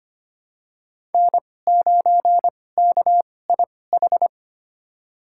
Morse code, heard as a keyed tone, about 25 wpm, spaced as for 18 wpm, 700 Hz. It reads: N9KIH
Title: USA Call Signs - 5 Characters - 2x Character Spacing - 25wpm